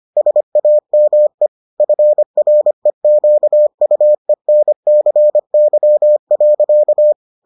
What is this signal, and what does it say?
Morse code, heard as a keyed tone, about 25 wpm, 600 Hz. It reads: SAME FREQUENCY.